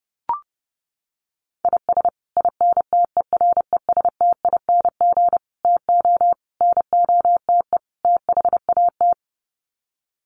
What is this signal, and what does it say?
Morse code, heard as a keyed tone, about 30 wpm, 700 Hz. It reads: IS INTERESTING TO NOTE THAT
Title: Sets of 5 Words Encore2 30wpm